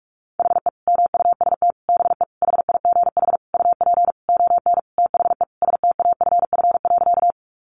Morse code, heard as a keyed tone, about 44 wpm, 700 Hz. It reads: HE MUST BE HIGH UP ON THE STAFF.
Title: Sentences from Top 1200 Words 45wpm